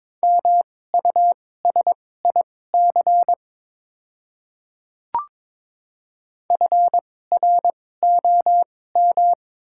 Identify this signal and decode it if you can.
Morse code, heard as a keyed tone, about 22 wpm, spaced as for 16 wpm, 700 Hz. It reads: MUSIC FROM
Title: 2x Character Spacing - Top 300 Words - 22wpm